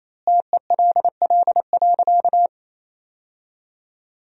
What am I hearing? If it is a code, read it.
Morse code, heard as a keyed tone, about 28 wpm, 700 Hz. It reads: TELL.
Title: 3x Word-Spacing - Sentences from Top 300 Words - 28wpm